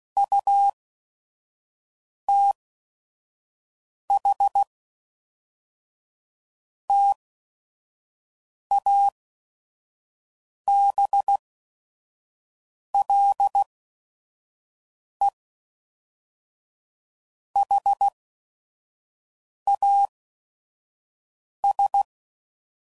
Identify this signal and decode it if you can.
Morse code, heard as a keyed tone, about 16 wpm, spaced as for 5 wpm, 800 Hz. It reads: UTH TABLE HAS
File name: CW Code Groups @5WPM / Technical Subjects / AND Gate